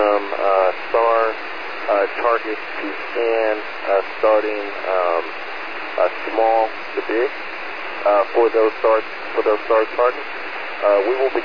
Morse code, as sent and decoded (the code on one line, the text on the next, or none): none